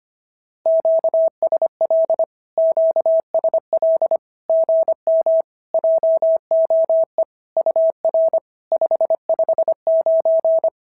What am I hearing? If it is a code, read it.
Morse code, heard as a keyed tone, about 25 wpm, 650 Hz. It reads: QSL QSL GM JOE UR 559
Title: SOTA QSO Practice 25wpm